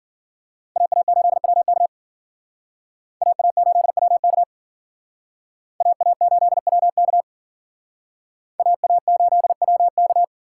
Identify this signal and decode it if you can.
Morse code, heard as a keyed tone, about 51 wpm, 700 Hz. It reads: AA8WK AA8WK AA8WK AA8WK